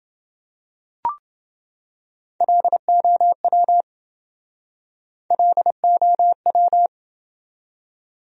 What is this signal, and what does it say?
Morse code, heard as a keyed tone, about 27 wpm, 700 Hz. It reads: LOW LOW